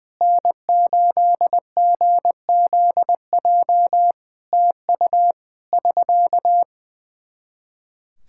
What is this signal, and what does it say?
Morse code, heard as a keyed tone, about 20 wpm, 700 Hz. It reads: N8GZJ TU <SK>
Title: Practice QSO - Exchange 4 - Ending - 20wpm